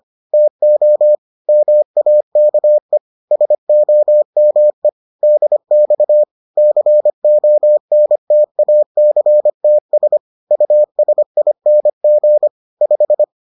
Morse code, heard as a keyed tone, about 25 wpm, 600 Hz. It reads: TO MAKE SOME DX CONTACTS USING 5